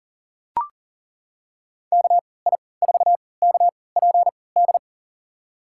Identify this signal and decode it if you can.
Morse code, heard as a keyed tone, about 40 wpm, spaced as for 23 wpm, 700 Hz. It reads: KI4KPD
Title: USA Call Signs - 6 Characters - 3x Character Spacing - 40wpm